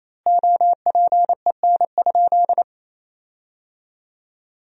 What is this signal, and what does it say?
Morse code, heard as a keyed tone, about 27 wpm, 700 Hz. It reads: OPEN?
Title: 3x Word-Spacing - Sentences from Top 300 Words - 28wpm